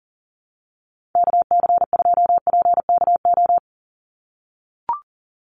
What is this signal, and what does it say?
Morse code, heard as a keyed tone, about 42 wpm, 700 Hz. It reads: KC2PKO
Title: USA Call Signs 40wpm